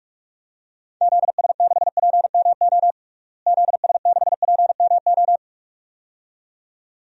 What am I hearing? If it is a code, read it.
Morse code, heard as a keyed tone, about 46 wpm, 700 Hz. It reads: ZS6PMO ZS6PMO
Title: DXCC-100 Call Signs - 6 Characters - Repeated 2x - 45wpm